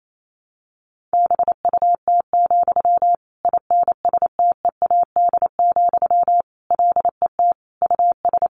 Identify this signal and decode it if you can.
Morse code, heard as a keyed tone, about 28 wpm, 700 Hz. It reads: BUT, INSTEAD, LET US
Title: Sentences from Top 1500 Words 28wpm